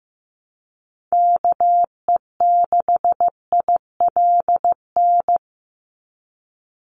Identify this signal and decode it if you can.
Morse code, heard as a keyed tone, about 15 wpm, 700 Hz. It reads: KE6ILN
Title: USA Call Signs 15wpm